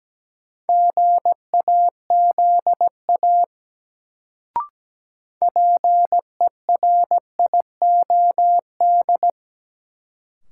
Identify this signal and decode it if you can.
Morse code, heard as a keyed tone, about 17 wpm, 700 Hz. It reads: GAZA PERIOD
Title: Sets of 5 Words - New Word Drill - 17wpm